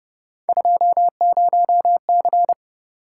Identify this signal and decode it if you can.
Morse code, heard as a keyed tone, about 29 wpm, 700 Hz. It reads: 20C